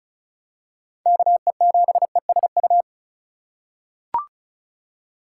Morse code, heard as a keyed tone, about 34 wpm, 700 Hz. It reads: KE7ESU
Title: USA Call Signs - 6 Characters - 2x Character Spacing - 35wpm